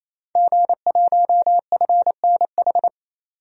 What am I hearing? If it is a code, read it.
Morse code, heard as a keyed tone, about 27 wpm, 700 Hz. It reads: G1FNH
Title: DXCC-100 Call Signs - 5 Characters - 28wpm